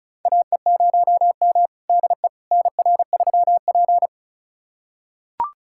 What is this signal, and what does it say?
Morse code, heard as a keyed tone, about 36 wpm, 700 Hz. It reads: AE0M DE NR3P
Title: USA Callsign DE Callsign - 4 Characters - 35wpm